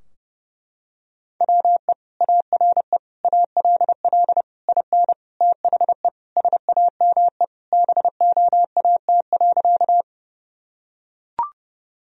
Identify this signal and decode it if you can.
Morse code, heard as a keyed tone, about 30 wpm, 700 Hz. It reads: WE ARE ALL IN THE SAME BOAT.
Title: Sentences from Top 1000 Words - Review Part 3 - 30wpm